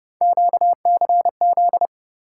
Morse code, heard as a keyed tone, about 31 wpm, 700 Hz. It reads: QCZ